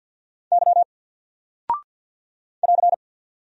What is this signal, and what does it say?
Morse code, heard as a keyed tone, about 48 wpm, 700 Hz. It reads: Y <AR>
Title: Single Character (Letter:Number:Punctuation:Prosign) Alt 50wpm